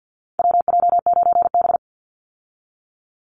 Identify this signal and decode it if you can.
Morse code, heard as a keyed tone, about 50 wpm, 700 Hz. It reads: WJ9B
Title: Call Signs - 50wpm